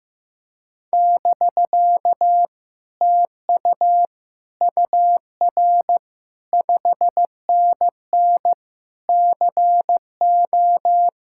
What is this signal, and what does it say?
Morse code, heard as a keyed tone, about 15 wpm, 700 Hz. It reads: <BK> TU UR 5NN CO
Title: Speed-Racing Common QSO Phrases Encore - 15wpm